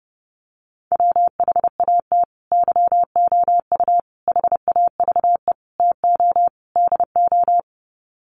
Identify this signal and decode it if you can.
Morse code, heard as a keyed tone, about 30 wpm, 700 Hz. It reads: WHAT YOU HAVE TO DO
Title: Sets of 5 Words Encore 30wpm